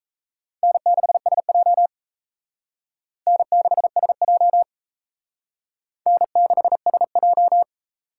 Morse code, heard as a keyed tone, about 36 wpm, 700 Hz. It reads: N6SJ N6SJ N6SJ